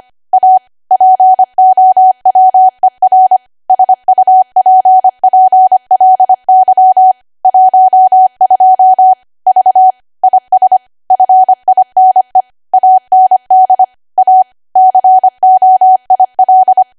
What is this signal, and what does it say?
Morse code, heard as a keyed tone, about 25 wpm, 750 Hz. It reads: A POWER SUPPLY 12 V IS FINE AND A COIL
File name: cwa / W1AW-Level3 / 25WPM